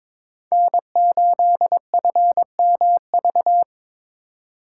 Morse code, heard as a keyed tone, about 26 wpm, 700 Hz. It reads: N8FMV
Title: Speed Racing - USA 1x3 Call Signs 22wpm